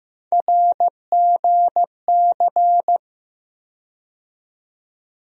Morse code, heard as a keyed tone, about 15 wpm, 700 Hz. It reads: RGC
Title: USA Call Sign Postfixes - 3 Letters 15wpm